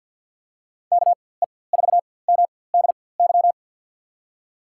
Morse code, heard as a keyed tone, about 53 wpm, spaced as for 24 wpm, 700 Hz. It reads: KE4MDY
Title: USA Call Signs - 4x Character Spacing - 50wpm